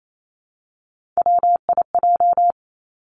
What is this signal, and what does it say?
Morse code, heard as a keyed tone, about 28 wpm, 700 Hz. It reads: WIJ
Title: Three Challenging Letter-Number Combinations 28wpm